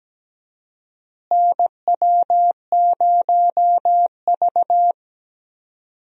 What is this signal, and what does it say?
Morse code, heard as a keyed tone, about 17 wpm, 700 Hz. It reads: NW0V
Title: USA Call Signs - 4 Characters - Repeated 2x - 17wpm